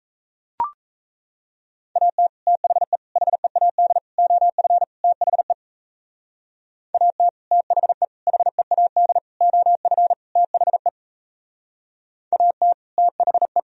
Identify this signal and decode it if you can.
Morse code, heard as a keyed tone, about 38 wpm, 700 Hz. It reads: AT THE HEAD OF THE AT THE HEAD OF THE AT THE